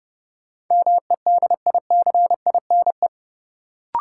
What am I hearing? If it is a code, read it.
Morse code, heard as a keyed tone, about 30 wpm, 700 Hz. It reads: MEDICINE
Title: Sentences from Top 900 Words - New Word Drill - 30wpm